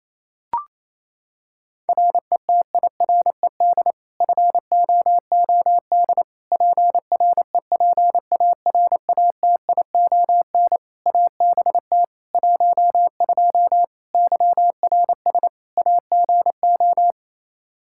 Morse code, heard as a keyed tone, about 28 wpm, 700 Hz. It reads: RETIRED FOOD PREPARATION ABT 12 YRS AGO
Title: K1USN SST QSO Practice 28wpm